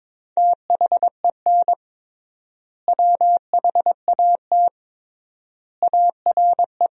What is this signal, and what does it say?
Morse code, heard as a keyed tone, about 22 wpm, 700 Hz. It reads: THEN WHAT ARE
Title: 3x Word-Spacing - Sentences from Top 100 Words - 22wpm